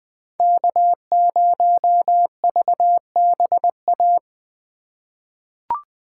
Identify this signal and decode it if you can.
Morse code, heard as a keyed tone, about 20 wpm, 700 Hz. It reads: K0VBA
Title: USA Call Signs - 4x Character Spacing - 20wpm